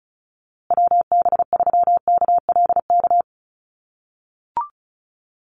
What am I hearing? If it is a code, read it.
Morse code, heard as a keyed tone, about 34 wpm, 700 Hz. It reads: WB3KLK
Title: USA Call Signs Repeated 3x 35wpm